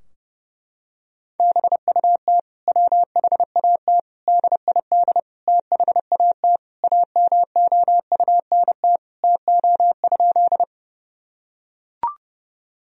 Morse code, heard as a keyed tone, about 30 wpm, 700 Hz. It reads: BUT WHAT DID THAT AMOUNT TO?
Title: Sentences from Top 1000 Words - Review Part 4 - 30wpm